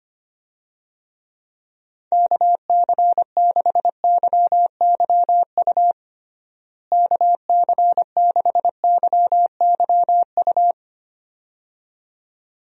Morse code, heard as a keyed tone, about 25 wpm, 700 Hz. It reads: KC6YYU KC6YYU
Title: USA Call Signs - 6 Characters - Repeated 2x - 25wpm